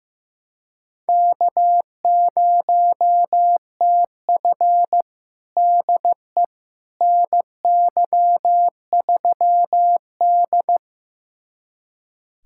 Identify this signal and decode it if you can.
Morse code, heard as a keyed tone, about 16 wpm, 700 Hz. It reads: K0TF DE NY3D
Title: USA Callsign DE Callsign - 4 Characters - 15wpm